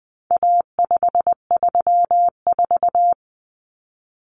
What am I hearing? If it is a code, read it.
Morse code, heard as a keyed tone, about 20 wpm, 700 Hz. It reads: A534
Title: Common Cut Numbers - 4-Digit 20wpm